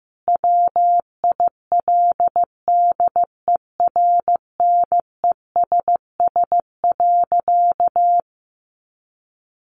Morse code, heard as a keyed tone, about 15 wpm, 700 Hz. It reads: WILDERNESS.
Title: A Sound Of Thunder 15wpm